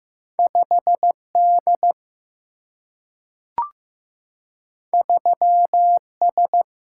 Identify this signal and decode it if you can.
Morse code, heard as a keyed tone, about 15 wpm, 700 Hz. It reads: 5 D 3 S
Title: Single-Letter Single-Number 15wpm